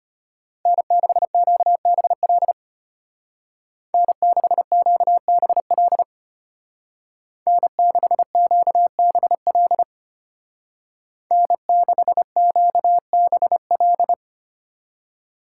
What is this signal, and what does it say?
Morse code, heard as a keyed tone, about 33 wpm, 700 Hz. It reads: N6QBL N6QBL N6QBL N6QBL